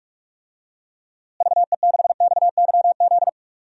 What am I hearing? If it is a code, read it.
Morse code, heard as a keyed tone, about 42 wpm, 700 Hz. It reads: VE6XYZ
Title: DXCC-100 Call Signs - 6 Characters - Repeated 3x - 45wpm